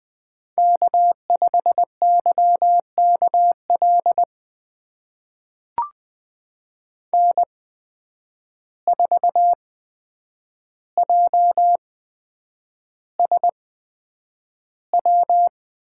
Morse code, heard as a keyed tone, about 20 wpm, spaced as for 9 wpm, 700 Hz. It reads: K5YKL N4JSW
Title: USA Call Signs - 5 Characters - 8x Character Spacing - 20wpm